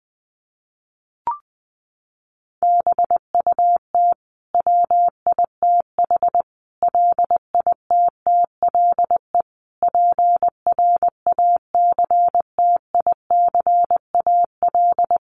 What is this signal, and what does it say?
Morse code, heard as a keyed tone, about 20 wpm, 700 Hz. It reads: BUT WITH LITTLE PRACTICAL
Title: Sentences from Top 1400 Words 20wpm